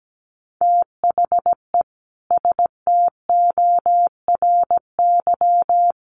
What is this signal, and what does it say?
Morse code, heard as a keyed tone, about 17 wpm, 700 Hz. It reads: THE STORY
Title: Sets of 3 Words 17wpm